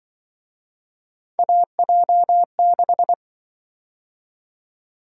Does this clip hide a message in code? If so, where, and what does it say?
Yes: Morse code from 1.4 to 3.1 s, heard as a keyed tone, about 26 wpm, 700 Hz. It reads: AJ6